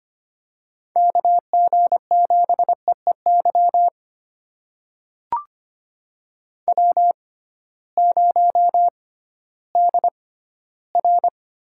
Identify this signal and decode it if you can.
Morse code, heard as a keyed tone, about 25 wpm, spaced as for 14 wpm, 700 Hz. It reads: KG7EEY W0DR
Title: USA Call Signs - 6x Character Spacing - 25wpm